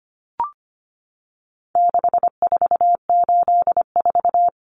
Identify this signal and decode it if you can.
Morse code, heard as a keyed tone, about 25 wpm, 700 Hz. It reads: 6484
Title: Numbers - 4-Digit 25wpm